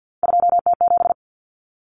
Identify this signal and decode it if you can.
Morse code, heard as a keyed tone, about 50 wpm, 700 Hz. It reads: 2T7
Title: Common Cut Numbers - 3-Digit 50wpm